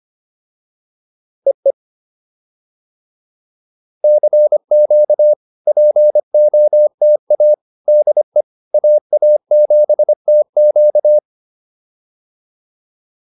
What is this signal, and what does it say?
Morse code, heard as a keyed tone, about 25 wpm, 600 Hz. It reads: EE CQ POTA DE AA7TQ